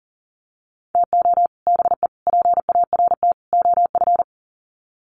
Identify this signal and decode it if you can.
Morse code, heard as a keyed tone, about 40 wpm, 700 Hz. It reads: TO BE PART OF
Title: Sets of 4 Words 40wpm